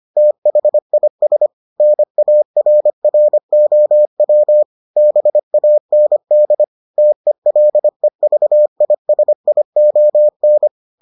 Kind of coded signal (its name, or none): Morse code